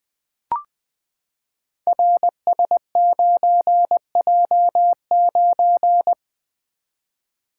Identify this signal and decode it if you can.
Morse code, heard as a keyed tone, about 20 wpm, 700 Hz. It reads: RS9J9
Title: Code Groups - 5 Letter:Numbers - 20wpm